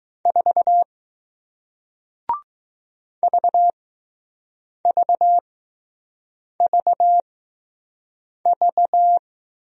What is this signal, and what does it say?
Morse code, heard as a keyed tone, about 20 wpm, spaced as for 6 wpm, 700 Hz. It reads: 4 VVVV